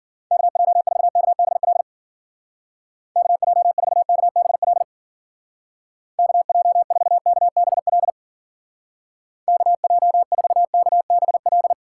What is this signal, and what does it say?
Morse code, heard as a keyed tone, about 50 wpm, 700 Hz. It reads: KJ4KBL KJ4KBL KJ4KBL KJ4KBL
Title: Speed Racing - USA 2x3 Call Signs 40wpm